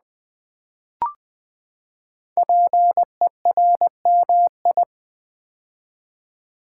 Morse code, heard as a keyed tone, about 20 wpm, 700 Hz. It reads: PERMI
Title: Build A Word - 5 Letters At A Time - 10-Letter Words - 20wpm